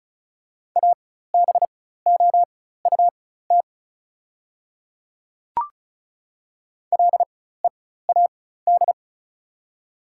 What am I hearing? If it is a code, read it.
Morse code, heard as a keyed tone, about 34 wpm, spaced as for 16 wpm, 690 Hz. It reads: ABOUT LEAD
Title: 4x Character Spacing - Top 500 Words - 35wpm